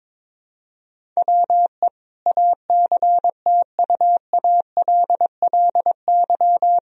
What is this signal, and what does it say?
Morse code, heard as a keyed tone, about 22 wpm, 700 Hz. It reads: WE ACTUALLY
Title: Sentences from Top 1000 Words - Review Part 4 - 22wpm